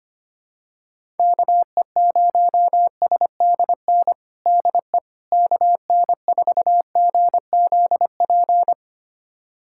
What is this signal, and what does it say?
Morse code, heard as a keyed tone, about 25 wpm, 700 Hz. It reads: KE0SDN DE KN4GZP
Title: USA Callsign DE Callsign - 6 Characters - 25wpm